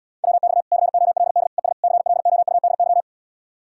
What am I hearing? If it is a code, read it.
Morse code, heard as a keyed tone, about 79 wpm, 700 Hz. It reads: WX COLD ES CLOUDY